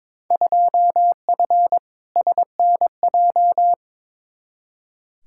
Morse code, heard as a keyed tone, about 22 wpm, 700 Hz. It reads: 2F SNJ
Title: Field Day Exchanges Encore - 22wpm